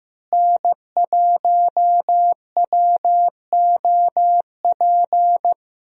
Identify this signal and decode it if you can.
Morse code, heard as a keyed tone, about 15 wpm, 700 Hz. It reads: N1WOP
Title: USA Call Signs - 5 Characters - Repeated 3x - 15wpm